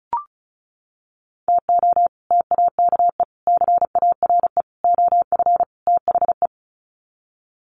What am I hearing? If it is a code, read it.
Morse code, heard as a keyed tone, about 34 wpm, 700 Hz. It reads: TO TAKE CARE OF THE
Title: Sets of 5 Words Encore 35wpm